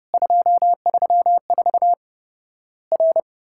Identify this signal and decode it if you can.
Morse code, heard as a keyed tone, about 30 wpm, 700 Hz. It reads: 234 R
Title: Worked All Europe DX Contest - QTC Exchange - 30wpm